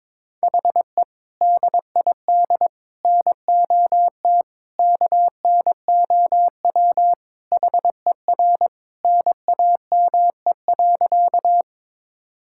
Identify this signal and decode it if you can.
Morse code, heard as a keyed tone, about 22 wpm, 700 Hz. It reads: HE DID NOT KNOW HER NAME.
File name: practice / 22wpm / Sentences with 4-Word Phrases - 22wpm